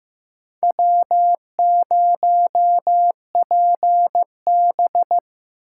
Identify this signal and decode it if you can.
Morse code, heard as a keyed tone, about 15 wpm, 700 Hz. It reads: W0PB
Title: USA Call Signs - 4 Characters - 4x Character Spacing - 15wpm